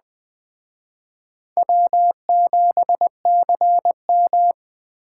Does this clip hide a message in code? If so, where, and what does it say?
Yes: Morse code from 1.6 to 4.5 s, heard as a keyed tone, about 20 wpm, 700 Hz. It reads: W7CM